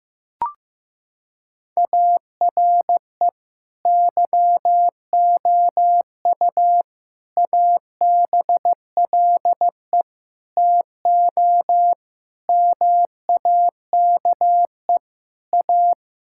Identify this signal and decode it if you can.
Morse code, heard as a keyed tone, about 15 wpm, 700 Hz. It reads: ARE YOU ABLE TO MAKE A
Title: Sentences from Top 1000 Words - Review Part 6 - 15wpm